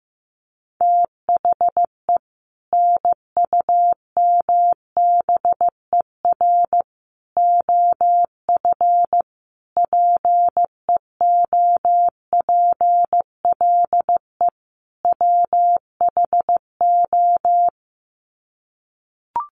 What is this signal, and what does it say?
Morse code, heard as a keyed tone, about 15 wpm, 700 Hz. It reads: THE NUMBER OF PEOPLE WHO
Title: Sets of 5 Words Encore 15wpm